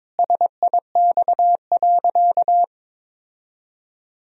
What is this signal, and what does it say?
Morse code, heard as a keyed tone, about 22 wpm, 700 Hz. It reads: SIX.
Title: Sentences from Top 1000 Words - Review Part 2 - 22wpm